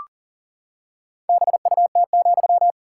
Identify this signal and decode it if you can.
Morse code, heard as a keyed tone, about 40 wpm, 700 Hz. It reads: BUT,
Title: Sentences from Top 1000 Words - Review Part 5 - 40wpm